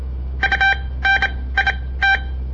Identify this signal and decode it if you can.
Morse code, heard as a keyed tone, about 27 wpm, spaced as for 18 wpm, 1550 Hz. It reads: UNIT
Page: On the air Morse code practice from the Southern Montana Amatuer Radio Association; all code, no read back